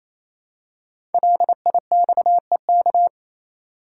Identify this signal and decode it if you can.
Morse code, heard as a keyed tone, about 28 wpm, 700 Hz. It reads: LIXEK
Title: Code Groups - 5 Letter:Numbers - 28wpm